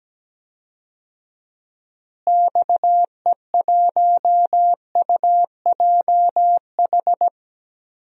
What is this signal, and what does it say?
Morse code, heard as a keyed tone, about 17 wpm, 700 Hz. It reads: XE1UJH
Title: DXCC-100 Call Signs - 6 Characters - Repeated 3x - 17wpm